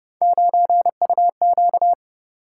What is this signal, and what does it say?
Morse code, heard as a keyed tone, about 30 wpm, 700 Hz. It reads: 9UQ
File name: Speed-Racing - Three Letter-Number Combinations - 25wpm